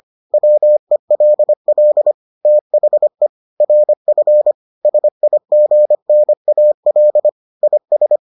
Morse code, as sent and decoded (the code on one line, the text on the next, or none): .-- . .-.. .-.. / - .... / .-. ..-. / ... .. --. -. .- .-.. / .. ...
WELL THE RF SIGNAL IS